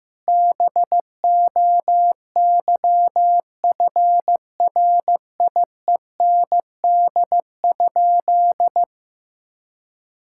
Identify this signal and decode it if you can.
Morse code, heard as a keyed tone, about 15 wpm, 700 Hz. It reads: BOYFRIEND?
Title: Sentences from Top 1000 Words - Review Part 5 - 15wpm